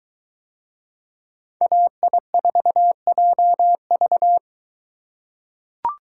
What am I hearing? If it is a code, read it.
Morse code, heard as a keyed tone, about 22 wpm, 700 Hz. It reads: AI4JV